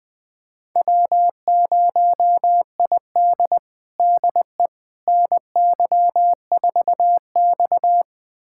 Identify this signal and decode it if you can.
Morse code, heard as a keyed tone, about 21 wpm, 700 Hz. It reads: W0ID DE NY4X
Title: USA Callsign DE Callsign - 4 Characters - 20wpm